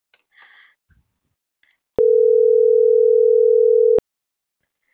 Ringback tone (North America), 440 + 480 Hz.